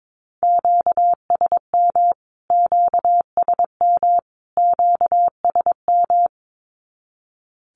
Morse code, heard as a keyed tone, about 22 wpm, 700 Hz. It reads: QSM QSM QSM
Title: Q-Codes Repeated 3x 22wpm